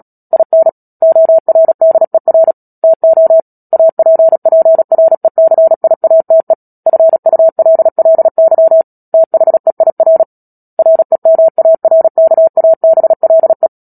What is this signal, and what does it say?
Morse code, heard as a keyed tone, about 37 wpm, 650 Hz. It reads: IN ORDER TO APPRECIATE FULLY THEIR REMARKABLE
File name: Book Two / 10-Chapter 10 - Epilogue - 36 WPM